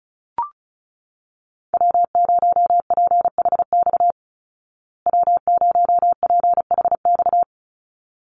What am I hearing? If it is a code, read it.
Morse code, heard as a keyed tone, about 34 wpm, 700 Hz. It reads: W0PHX W0PHX
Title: Call Signs Repeated 3x - 35wpm